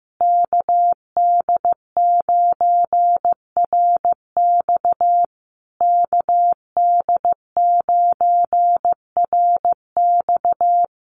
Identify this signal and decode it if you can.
Morse code, heard as a keyed tone, about 15 wpm, 700 Hz. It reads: KD9RX KD9RX
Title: USA Call Signs Repeated 3x 15wpm